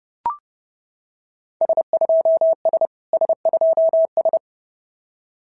Morse code, heard as a keyed tone, about 30 wpm, 650 Hz. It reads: S2S S2S
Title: SOTA QSO Practice 30wpm